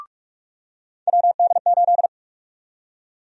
Morse code, heard as a keyed tone, about 43 wpm, 700 Hz. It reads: WD8
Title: USA Call Sign Prefixes - 45wpm